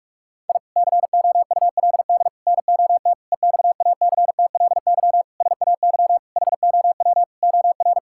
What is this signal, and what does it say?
Morse code, heard as a keyed tone, about 44 wpm, 700 Hz. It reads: I COULD NOT EXACTLY SAY HOW OR
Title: Sentences from Top 1000 Words - Review Part 6 - 45wpm